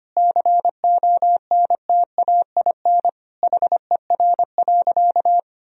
Morse code, heard as a keyed tone, about 25 wpm, 700 Hz. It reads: CONTAIN HER.
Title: Sentences from Top 1000 Words - Review Part 2 - 25wpm